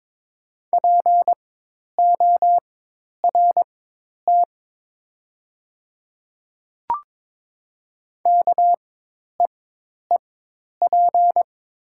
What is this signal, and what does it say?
Morse code, heard as a keyed tone, about 22 wpm, spaced as for 10 wpm, 700 Hz. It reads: PORT KEEP
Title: 4x Character Spacing - Top 500 Words - 22wpm